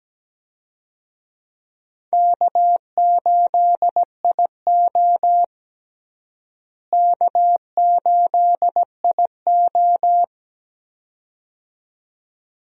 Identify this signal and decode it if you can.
Morse code, heard as a keyed tone, about 17 wpm, 700 Hz. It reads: K8IO K8IO
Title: USA Call Signs - 4 Characters - Repeated 2x - 17wpm